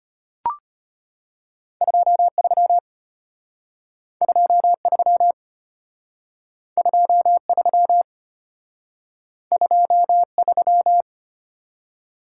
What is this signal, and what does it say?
Morse code, heard as a keyed tone, about 31 wpm, 700 Hz. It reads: 23 23 23 23
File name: Speed-Racing - Two Letter-Number Combinations - 25wpm